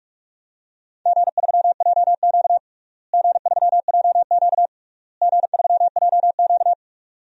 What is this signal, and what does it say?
Morse code, heard as a keyed tone, about 46 wpm, 700 Hz. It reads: G3JQ G3JQ G3JQ
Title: DXCC-100 Call Signs - 4 Characters - Repeated 3x - 45wpm